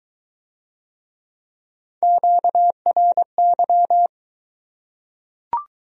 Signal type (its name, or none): Morse code